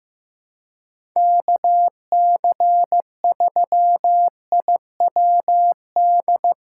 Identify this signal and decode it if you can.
Morse code, heard as a keyed tone, about 15 wpm, 700 Hz. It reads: KC3IWD
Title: USA Callsign DE Callsign - 6 Characters - 15wpm